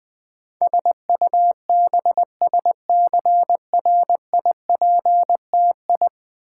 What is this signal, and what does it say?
Morse code, heard as a keyed tone, about 20 wpm, 700 Hz. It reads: SUBSCRIPTI
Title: Build A Word - 5 Letters At A Time - 12-Letter Words - 20wpm